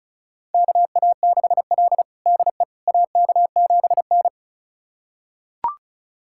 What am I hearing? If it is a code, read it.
Morse code, heard as a keyed tone, about 34 wpm, 700 Hz. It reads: KA6L DE AK7N
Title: USA Callsign DE Callsign - 4 Characters - 35wpm